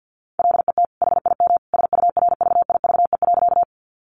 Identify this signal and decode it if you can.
Morse code, heard as a keyed tone, about 50 wpm, 700 Hz. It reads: LET HIM SURVIVE.
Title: Sentences from Top 1300 Words 50wpm